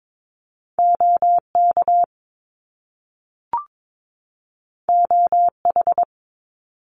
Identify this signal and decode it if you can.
Morse code, heard as a keyed tone, about 22 wpm, 700 Hz. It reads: OK OH
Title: Top Two Letter Combinations in Words 22wpm